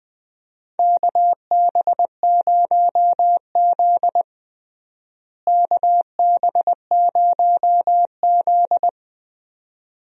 Morse code, heard as a keyed tone, about 20 wpm, 700 Hz. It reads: KB0Z KB0Z
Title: USA Call Signs - 4 Characters - Repeated 2x - 20wpm